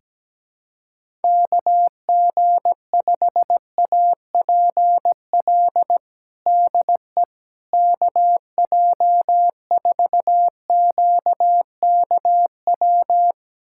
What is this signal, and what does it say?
Morse code, heard as a keyed tone, about 17 wpm, 700 Hz. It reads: KG5APL DE KJ4QKW
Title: USA Callsign DE Callsign 17wpm